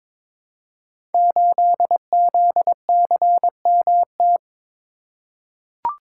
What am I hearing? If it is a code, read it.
Morse code, heard as a keyed tone, about 22 wpm, 700 Hz. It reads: 8ZCMT
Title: Code Groups - 5 Letter:Numbers - 22wpm